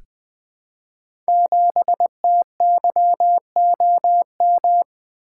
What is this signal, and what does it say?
Morse code, heard as a keyed tone, about 20 wpm, 700 Hz. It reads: 7TYOM
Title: Code Groups - 5 Letter:Numbers - 20wpm